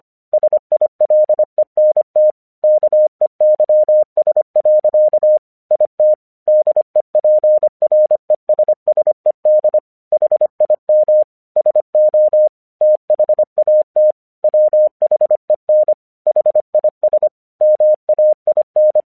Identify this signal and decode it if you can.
Morse code, heard as a keyed tone, about 25 wpm, 610 Hz. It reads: SILENT KEYS. IT DEPRESSED HIM SO THAT WHEN HIS MAIN
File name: 1 English 25wpm / AQC13